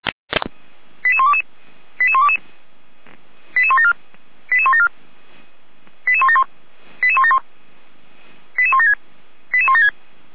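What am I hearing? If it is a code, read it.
Five-tone selective call (ZVEI), heard as quick runs of five tones, tones of about 70 ms. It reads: 88122 88122 88174 88174 88171 88171 88176 88176
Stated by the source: ZVEI five-tone selective call received with an SDR.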